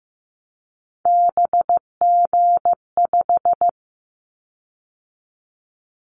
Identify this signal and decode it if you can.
Morse code, heard as a keyed tone, about 15 wpm, 700 Hz. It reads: BG5
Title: Callsign Prefixes for Top 100 DXCC - 15wpm